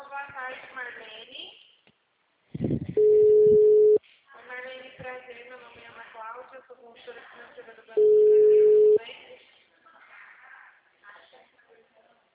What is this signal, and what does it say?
Ringback tone (Europe, 425 Hz), with speech.